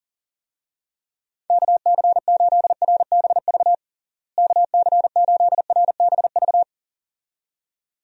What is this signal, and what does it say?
Morse code, heard as a keyed tone, about 40 wpm, 700 Hz. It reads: KC8RBV KC8RBV
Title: USA Call Signs - 6 Characters - Repeated 2x - 40wpm